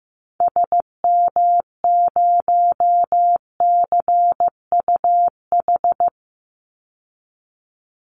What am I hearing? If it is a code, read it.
Morse code, heard as a keyed tone, about 15 wpm, 700 Hz. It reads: SM0CUH
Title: Call Signs Repeated 3x - 15wpm